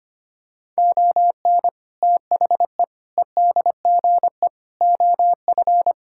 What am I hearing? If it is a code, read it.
Morse code, heard as a keyed tone, about 25 wpm, 700 Hz. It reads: ON THE EDGE OF
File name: practice / 25wpm / Sentences with 4-Word Phrases - 25wpm